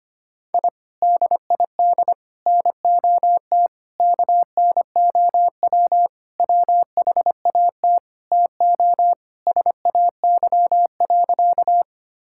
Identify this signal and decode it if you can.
Morse code, heard as a keyed tone, about 25 wpm, 700 Hz. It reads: I DID NOT KNOW WHAT TO SAY.
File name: practice / 25wpm / Sentences with 4-Word Phrases - 25wpm